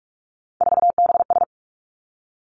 Morse code, heard as a keyed tone, about 46 wpm, 700 Hz. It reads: 4BS
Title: Three Challenging Letter-Number Combinations 45wpm